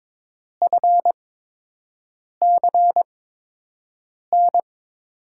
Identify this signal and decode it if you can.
Morse code, heard as a keyed tone, about 22 wpm, spaced as for 6 wpm, 700 Hz. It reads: FCN
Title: USA Call Signs - 6 Characters - 8x Character Spacing - 22wpm